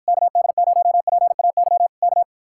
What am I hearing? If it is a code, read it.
Morse code, heard as a keyed tone, about 52 wpm, 700 Hz. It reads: KD0PAY K